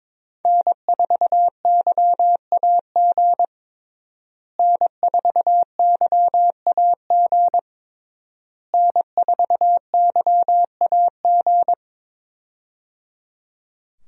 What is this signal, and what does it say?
Morse code, heard as a keyed tone, about 21 wpm, 700 Hz. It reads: N4YAG N4YAG N4YAG
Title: USA Call Signs - 5 Characters - Repeated 3x - 22wpm